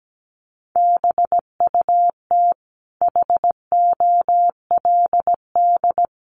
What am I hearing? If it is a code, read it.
Morse code, heard as a keyed tone, about 17 wpm, 700 Hz. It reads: BUT HOLD
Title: Sentences from Top 1300 Words 17wpm